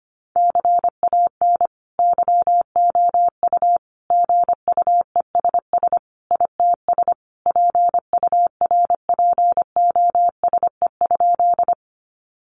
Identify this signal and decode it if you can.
Morse code, heard as a keyed tone, about 25 wpm, 700 Hz. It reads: CAN YOU GUESS ITS PURPOSE?
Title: Sentences from Top 1500 Words 25wpm